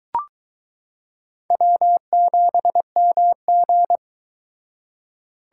Morse code, heard as a keyed tone, about 21 wpm, 700 Hz. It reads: W7MG